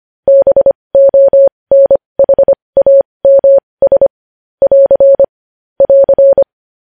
Morse code, heard as a keyed tone, about 25 wpm, spaced as for 20 wpm, 550 Hz. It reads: BONHAMS <AR> <AR>
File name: podcast / News headlines-1x-25x20